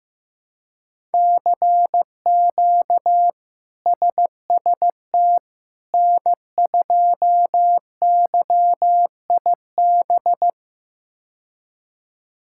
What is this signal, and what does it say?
Morse code, heard as a keyed tone, about 15 wpm, 700 Hz. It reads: CQ SST N2YIB
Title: K1USN SST QSO Practice - 15wpm